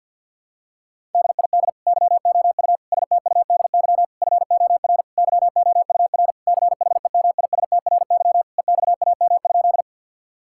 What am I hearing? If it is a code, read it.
Morse code, heard as a keyed tone, about 50 wpm, 700 Hz. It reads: DID YOU STUDY FOR YOUR CHEMISTRY EXAM?
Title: Sentences from Top 1000 Words - Review Part 6 - 50wpm